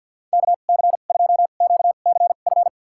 Morse code, heard as a keyed tone, about 51 wpm, spaced as for 36 wpm, 700 Hz. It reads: KC2QCF
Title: USA Call Signs - 2x Character Spacing - 50wpm